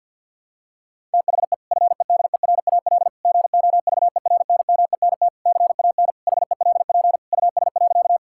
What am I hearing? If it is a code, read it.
Morse code, heard as a keyed tone, about 52 wpm, 700 Hz. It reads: THE FEDERAL GOVERNMENT CAN HELP US.
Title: Sentences from Top 1000 Words - Review Part 5 - 50wpm